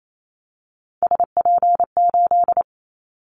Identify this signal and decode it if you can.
Morse code, heard as a keyed tone, about 28 wpm, 700 Hz. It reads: SP8
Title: Callsign Prefixes for Top 25 DXCC - 28wpm